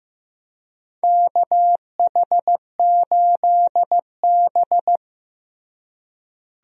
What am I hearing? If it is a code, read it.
Morse code, heard as a keyed tone, about 15 wpm, 700 Hz. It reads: KH8B
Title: USA Call Signs - 4 Characters - Repeated 2x - 15wpm